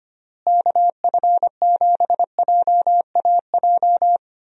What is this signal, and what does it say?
Morse code, heard as a keyed tone, about 25 wpm, 700 Hz. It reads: KF7JAJ
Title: USA Call Signs - 6 Characters - Repeated 2x - 25wpm